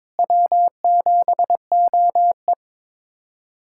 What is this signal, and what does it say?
Morse code, heard as a keyed tone, about 21 wpm, 700 Hz. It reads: W7OE
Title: USA Callsign DE Callsign - 4 Characters - 22wpm